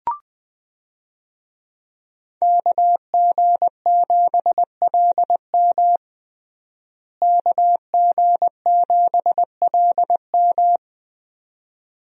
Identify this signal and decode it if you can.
Morse code, heard as a keyed tone, about 20 wpm, 700 Hz. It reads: KG7LM KG7LM
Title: USA Call Signs - 5 Characters - Repeated 2x - 20wpm